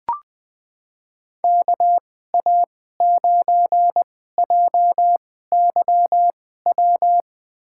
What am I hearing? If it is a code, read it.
Morse code, heard as a keyed tone, about 20 wpm, spaced as for 14 wpm, 700 Hz. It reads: KA9JYW
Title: USA Call Signs - 6 Characters - 2x Character Spacing - 20wpm